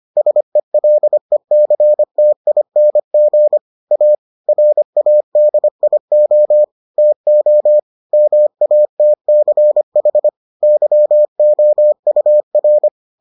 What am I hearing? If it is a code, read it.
Morse code, heard as a keyed tone, about 25 wpm, 600 Hz. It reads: SELECTING A RADIO TO MATCH YOUR